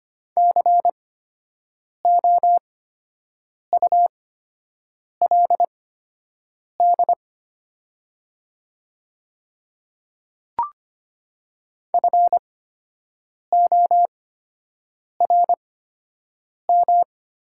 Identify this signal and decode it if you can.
Morse code, heard as a keyed tone, about 25 wpm, spaced as for 7 wpm, 700 Hz. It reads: COULD FORM
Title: 8x Character Spacing - Top 200 Words - 25wpm